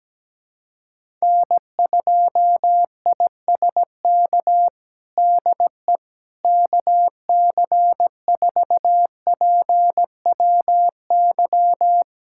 Morse code, heard as a keyed tone, about 17 wpm, 700 Hz. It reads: N2ISK DE KC4PWY
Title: USA Callsign DE Callsign 17wpm